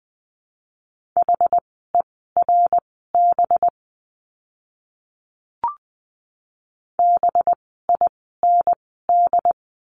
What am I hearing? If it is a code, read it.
Morse code, heard as a keyed tone, about 20 wpm, spaced as for 14 wpm, 700 Hz. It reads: HERB BIND
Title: 185 Lesson 30f - Letter b - Words 20wpm